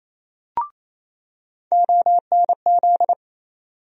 Morse code, heard as a keyed tone, about 28 wpm, 700 Hz. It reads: ONZ
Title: USA Call Sign Postfixes - 3 Letters 28wpm